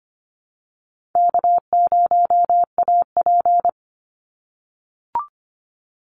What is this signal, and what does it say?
Morse code, heard as a keyed tone, about 25 wpm, 700 Hz. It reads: K0AP
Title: Call Signs - 25wpm